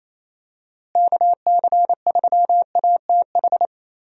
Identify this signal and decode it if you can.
Morse code, heard as a keyed tone, about 28 wpm, 700 Hz. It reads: KC3ATH
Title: USA Call Signs - 6 Characters - Repeated 2x - 28wpm